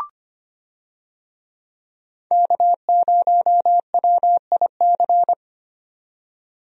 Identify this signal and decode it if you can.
Morse code, heard as a keyed tone, about 25 wpm, 700 Hz. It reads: K0WIC
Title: USA Call Signs - 5 Characters - 25wpm